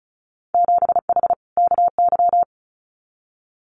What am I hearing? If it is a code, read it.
Morse code, heard as a keyed tone, about 36 wpm, 700 Hz. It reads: 7H KY
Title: Winter Field Day Exchanges Encore 35wpm